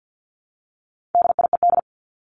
Morse code, heard as a keyed tone, about 50 wpm, 700 Hz. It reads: DIED
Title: ED Suffix - Top 100 Words with Base Words 50wpm